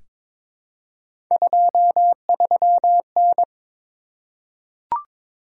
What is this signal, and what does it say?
Morse code, heard as a keyed tone, about 22 wpm, 700 Hz. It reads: 23N
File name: practice / 22wpm / Three Letter-Number Combinations - 22wpm